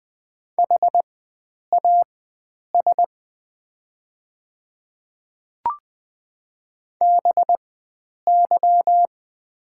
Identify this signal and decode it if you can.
Morse code, heard as a keyed tone, about 20 wpm, spaced as for 9 wpm, 700 Hz. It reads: HAS BY
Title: 4x Character Spacing - Top 100 Words - 20wpm